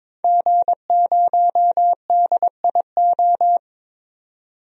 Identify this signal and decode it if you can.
Morse code, heard as a keyed tone, about 21 wpm, 700 Hz. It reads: G0DIO
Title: DXCC-100 Call Signs - 5 Characters - 22wpm